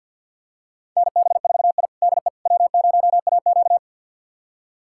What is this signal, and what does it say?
Morse code, heard as a keyed tone, about 46 wpm, 700 Hz. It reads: NB4I DE W0AQ
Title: USA Callsign DE Callsign - 4 Characters - 50wpm